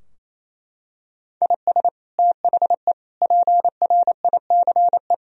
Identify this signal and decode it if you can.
Morse code, heard as a keyed tone, about 28 wpm, 700 Hz. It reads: IS THE PRICE
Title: Sentences from Top 1000 Words - Review Part 4 - 28wpm